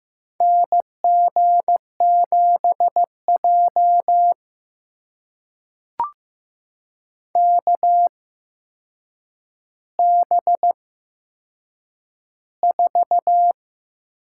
Morse code, heard as a keyed tone, about 15 wpm, spaced as for 7 wpm, 700 Hz. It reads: NG7J KB4
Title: USA Call Signs - 8x Character Spacing - 15wpm